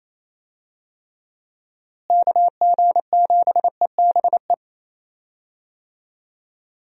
Morse code, heard as a keyed tone, about 28 wpm, 700 Hz. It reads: KG7EBE
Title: USA Call Signs - 6 Characters - Repeated 3x - 28wpm